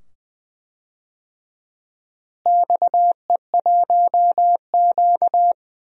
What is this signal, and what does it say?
Morse code, heard as a keyed tone, about 20 wpm, 700 Hz. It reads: XE1Q
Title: DXCC-100 Call Signs - 4 Characters - Repeated 2x - 20wpm